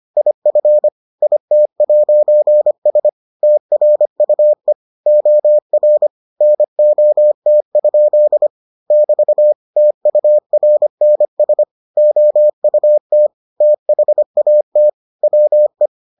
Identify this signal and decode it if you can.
Morse code, heard as a keyed tone, about 25 wpm, 600 Hz. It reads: IF IT'S TRUE OR NOT? <BT> TURNS OUT THAT WE